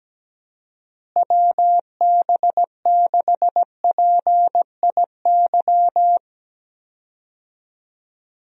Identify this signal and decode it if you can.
Morse code, heard as a keyed tone, about 17 wpm, 700 Hz. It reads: WB6PIY
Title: USA Call Signs - 6 Characters - Repeated 2x - 17wpm